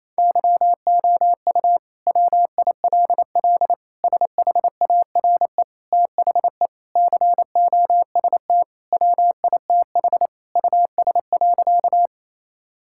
Morse code, heard as a keyed tone, about 28 wpm, 700 Hz. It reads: YOU WILL SHARE THE COST WITH US.
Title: Sentences from Top 1000 Words - Review Part 6 - 28wpm